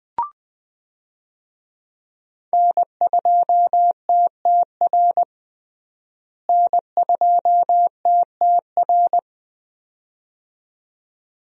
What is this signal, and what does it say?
Morse code, heard as a keyed tone, about 20 wpm, 700 Hz. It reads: N2TTR N2TTR
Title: USA Call Signs - 5 Characters - Repeated 2x - 20wpm